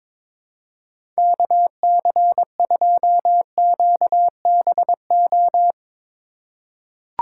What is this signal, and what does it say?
Morse code, heard as a keyed tone, about 22 wpm, 700 Hz. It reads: KC2QBO
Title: USA Call Signs - 6 Characters - 2x Character Spacing - 22wpm